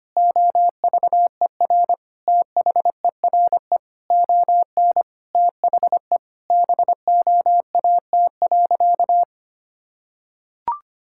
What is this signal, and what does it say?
Morse code, heard as a keyed tone, about 25 wpm, 700 Hz. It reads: OVER THERE ON THE BOAT.
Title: Sentences from Top 1000 Words - Review Part 5 - 25wpm